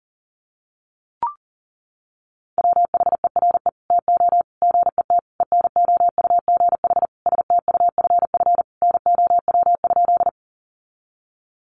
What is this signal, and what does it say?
Morse code, heard as a keyed tone, about 40 wpm, 700 Hz. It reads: WHERE TO GET ENOUGH STUFF NOW?
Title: Sentences from Top 1400 Words 40wpm